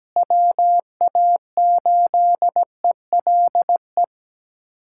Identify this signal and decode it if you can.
Morse code, heard as a keyed tone, about 17 wpm, 700 Hz. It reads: WA8ELE
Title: USA Callsign DE Callsign - 6 Characters - 17wpm